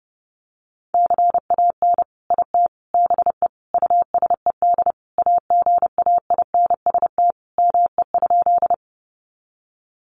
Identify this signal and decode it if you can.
Morse code, heard as a keyed tone, about 30 wpm, 700 Hz. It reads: CAN IT BE USED AGAINST ME?
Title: Sentences from Top 600 Words 30wpm